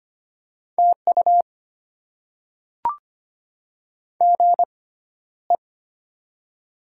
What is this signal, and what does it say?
Morse code, heard as a keyed tone, about 25 wpm, spaced as for 13 wpm, 700 Hz. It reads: TU GE
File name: Top 20 QSO Elements - 6x Character Spacing - 25wpm